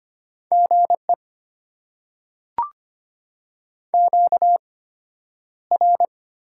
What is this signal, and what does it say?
Morse code, heard as a keyed tone, about 25 wpm, spaced as for 11 wpm, 700 Hz. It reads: GE QR
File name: Top 90 QSO Elements - 8x Character Spacing - 25wpm